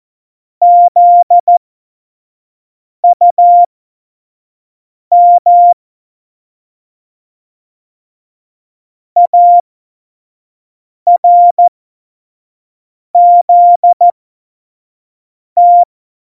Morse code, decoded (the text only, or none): ZUM ARZT